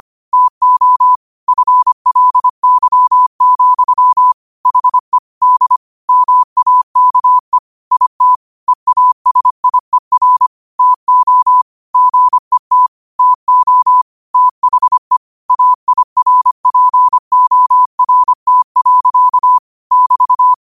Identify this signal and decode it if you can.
Morse code, heard as a keyed tone, about 25 wpm, 1000 Hz. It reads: TO FLY, HE D MAKE IT EASIER TO GET TO THE AIRPORT. <BT>